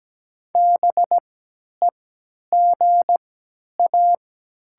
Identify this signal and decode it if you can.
Morse code, heard as a keyed tone, about 17 wpm, spaced as for 10 wpm, 700 Hz. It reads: BEGA
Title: 3x Character Spacing - Top 300 Words - 17wpm